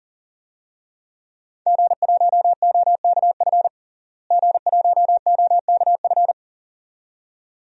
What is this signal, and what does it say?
Morse code, heard as a keyed tone, about 40 wpm, 700 Hz. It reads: G1OKF G1OKF
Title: DXCC-100 Call Signs - 5 Characters - Repeated 2x - 40wpm